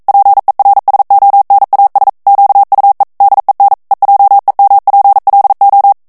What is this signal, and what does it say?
Morse code, heard as a keyed tone, about 42 wpm, 800 Hz. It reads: PERSONAS QUE DEN EJEMPLO
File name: morse mp3 / ESPAGNOL TEXTE EN CLAIR 1 VITESSE 40 MOTS MINUTES